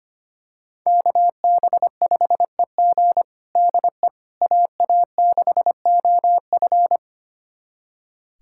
Morse code, heard as a keyed tone, about 25 wpm, 700 Hz. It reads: KB5EG DE AA6OF
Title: USA Callsign DE Callsign - 5 Characters - 25wpm